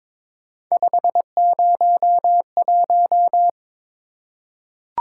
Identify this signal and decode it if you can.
Morse code, heard as a keyed tone, about 22 wpm, 700 Hz. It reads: 501
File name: practice / 22wpm / Three Letter-Number Combinations - 22wpm